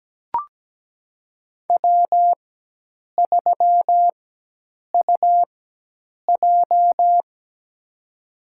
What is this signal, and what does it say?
Morse code, heard as a keyed tone, about 17 wpm, spaced as for 8 wpm, 700 Hz. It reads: W3UJ
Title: USA Call Signs - 4 Characters - 4x Character Spacing - 17wpm